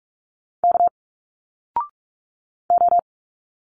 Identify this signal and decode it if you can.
Morse code, heard as a keyed tone, about 44 wpm, 700 Hz. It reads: K O
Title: Single Letters 45wpm